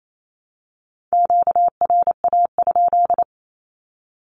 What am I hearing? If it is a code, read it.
Morse code, heard as a keyed tone, about 28 wpm, 700 Hz. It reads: QRA?
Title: Q-Codes 28wpm